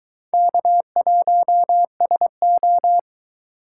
Morse code, heard as a keyed tone, about 23 wpm, 700 Hz. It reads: K1SO